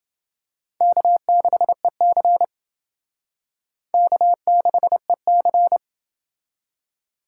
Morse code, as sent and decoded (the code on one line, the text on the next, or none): -.- -.... -.-. / -.- -.... -.-.
K6EC K6EC